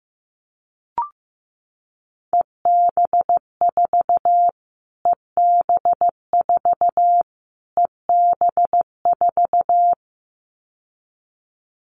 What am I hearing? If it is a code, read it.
Morse code, heard as a keyed tone, about 15 wpm, 700 Hz. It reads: EB4 EB4 EB4
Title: Callsign Prefixes for Top 25 DXCC - Repeated 3x - 15wpm